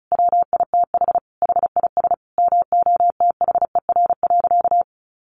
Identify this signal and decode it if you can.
Morse code, heard as a keyed tone, about 34 wpm, 700 Hz. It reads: WITH HIS MOTHER.
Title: Sentences from Top 1300 Words 35wpm